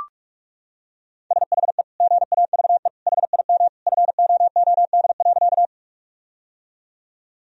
Morse code, heard as a keyed tone, about 43 wpm, 700 Hz. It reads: SHE GAVE HIM FOOD.
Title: Sentences from Top 1000 Words - Review Part 1 - 45wpm